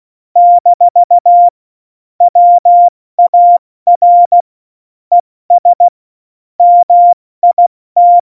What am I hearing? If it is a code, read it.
Morse code, heard as a keyed tone, about 16 wpm, spaced as for 14 wpm, 700 Hz. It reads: - WAR ES MIT
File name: Morsekurs / Witze / Witz 24. Oktober 17, Tempo 70